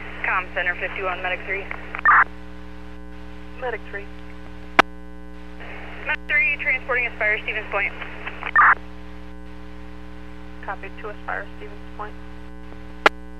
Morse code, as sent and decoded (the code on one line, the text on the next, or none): none